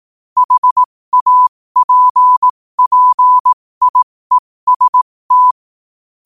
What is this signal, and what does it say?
Morse code, heard as a keyed tone, about 18 wpm, spaced as for 15 wpm, 1000 Hz. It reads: HAPPIEST
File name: qotdfiles / 15wpm